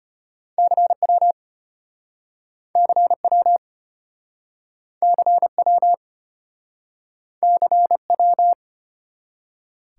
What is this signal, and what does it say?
Morse code, heard as a keyed tone, about 33 wpm, 700 Hz. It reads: CW CW CW CW